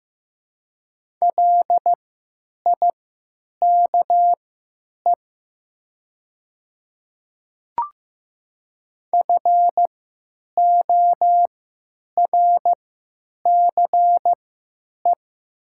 Morse code, heard as a keyed tone, about 15 wpm, spaced as for 9 wpm, 700 Hz. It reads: LIKE FORCE
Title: 3x Character Spacing - Top 500 Words - 15wpm